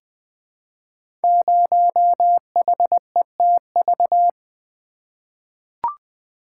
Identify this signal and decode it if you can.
Morse code, heard as a keyed tone, about 20 wpm, 700 Hz. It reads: 0HETV
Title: Code Groups - 5 Letter:Numbers - 20wpm